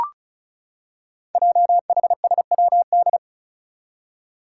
Morse code, heard as a keyed tone, about 36 wpm, 690 Hz. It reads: JHSWD
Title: Code Groups - 5 Letters - 35wpm